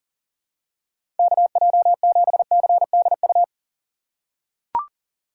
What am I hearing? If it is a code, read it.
Morse code, heard as a keyed tone, about 40 wpm, 700 Hz. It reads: KJ7CDU
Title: USA Call Signs - 6 Characters - 4x Character Spacing - 40wpm